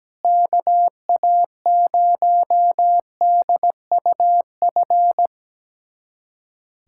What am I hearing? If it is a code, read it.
Morse code, heard as a keyed tone, about 17 wpm, 700 Hz. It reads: KA0DUF
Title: USA Callsign DE Callsign 17wpm